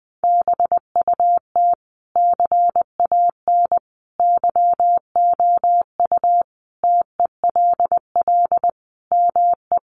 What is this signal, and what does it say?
Morse code, heard as a keyed tone, about 20 wpm, 700 Hz. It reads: BUT CAN YOU TELL ME
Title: Sentences from Top 1400 Words 20wpm